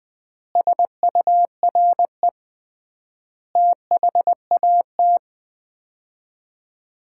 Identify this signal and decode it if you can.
Morse code, heard as a keyed tone, about 20 wpm, 700 Hz. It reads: SURE THAT
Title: 3x Word-Spacing - Sets of 5 Words - 20wpm